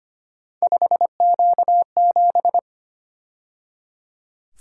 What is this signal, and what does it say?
Morse code, heard as a keyed tone, about 28 wpm, 700 Hz. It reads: 5Q7
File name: Speed-Racing - Three Letter-Number Combinations - 25wpm